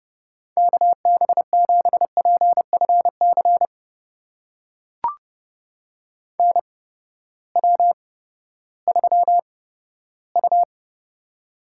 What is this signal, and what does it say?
Morse code, heard as a keyed tone, about 29 wpm, spaced as for 15 wpm, 700 Hz. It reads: KB7PFC NW3U
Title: USA Call Signs - 8x Character Spacing - 30wpm